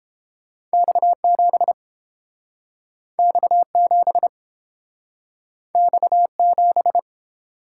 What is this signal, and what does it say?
Morse code, heard as a keyed tone, about 31 wpm, 700 Hz. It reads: X7 X7 X7